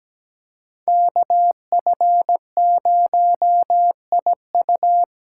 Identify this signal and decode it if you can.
Morse code, heard as a keyed tone, about 17 wpm, 700 Hz. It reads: KF0IU
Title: USA Call Signs - 8x Character Spacing - 17wpm